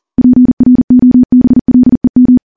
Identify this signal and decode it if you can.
Morse code, heard as a keyed tone, about 39 wpm, 250 Hz. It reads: PROBLEM
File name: morse / morse-training-40WPM